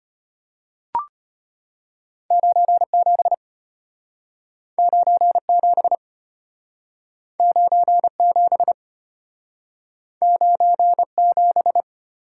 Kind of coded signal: Morse code

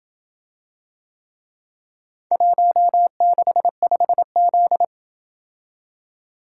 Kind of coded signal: Morse code

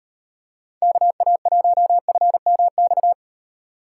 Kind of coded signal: Morse code

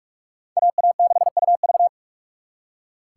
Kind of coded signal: Morse code

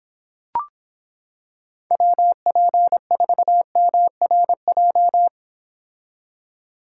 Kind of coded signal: Morse code

